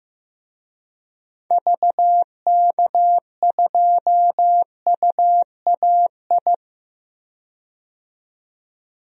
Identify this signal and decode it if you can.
Morse code, heard as a keyed tone, about 15 wpm, 700 Hz. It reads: VK2UAI
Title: DXCC-100 Call Signs - 6 Characters - 15wpm